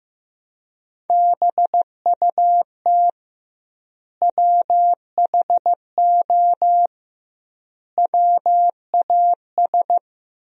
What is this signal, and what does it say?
Morse code, heard as a keyed tone, about 15 wpm, 700 Hz. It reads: BUT WHO WAS
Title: 2x Word-Spacing - Sentences from Top 200 Words - 15wpm